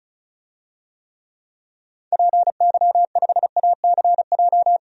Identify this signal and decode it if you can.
Morse code, heard as a keyed tone, about 36 wpm, 700 Hz. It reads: PY5ACJ
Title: DXCC-100 Call Signs - 6 Characters - 35wpm